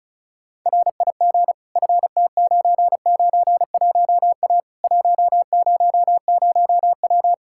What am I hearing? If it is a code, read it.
Morse code, heard as a keyed tone, about 34 wpm, 700 Hz. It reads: RIG FT991A 100W
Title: Practice QSO - Exchange 2 - Info - 35wpm